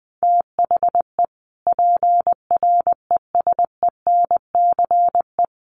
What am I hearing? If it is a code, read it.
Morse code, heard as a keyed tone, about 19 wpm, 700 Hz. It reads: THE PRESENCE
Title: Sets of 3 Words 20wpm